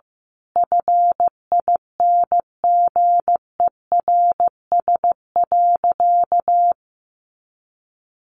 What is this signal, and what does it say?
Morse code, heard as a keyed tone, about 15 wpm, 700 Hz. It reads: FINGERS.
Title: A Sound Of Thunder 15wpm